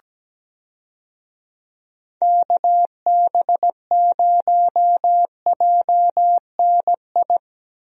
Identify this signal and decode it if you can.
Morse code, heard as a keyed tone, about 17 wpm, 700 Hz. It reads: KB0JNI
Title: USA Call Signs - 6 Characters - Repeated 2x - 17wpm